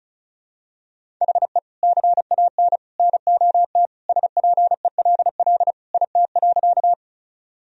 Morse code, heard as a keyed tone, about 36 wpm, 700 Hz. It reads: HE CAN NOT SPELL IT.
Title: Sentences from Top 1000 Words - Review Part 1 - 35wpm